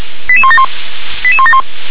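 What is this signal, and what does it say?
Five-tone selective call (ZVEI), heard as quick runs of five tones, tones of about 70 ms. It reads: 88171 88171
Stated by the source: ZVEI five-tone selective call, received by SDR.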